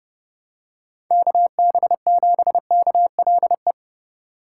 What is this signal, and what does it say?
Morse code, heard as a keyed tone, about 30 wpm, 700 Hz. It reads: KB7KLE
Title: USA Call Signs - 6 Characters - 2x Character Spacing - 30wpm